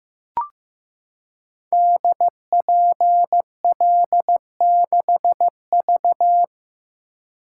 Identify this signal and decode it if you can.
Morse code, heard as a keyed tone, about 15 wpm, 700 Hz. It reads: DPL6V
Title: Code Groups - 5 Letter:Numbers - 15wpm